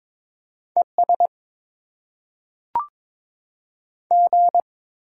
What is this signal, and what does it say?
Morse code, heard as a keyed tone, about 22 wpm, 700 Hz. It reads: ES G
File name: Top 20 QSO Elements - 6x Character Spacing - 22wpm